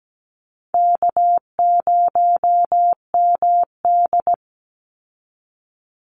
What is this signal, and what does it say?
Morse code, heard as a keyed tone, about 17 wpm, 700 Hz. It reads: K0MD
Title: Call Signs Repeated 2x - 17wpm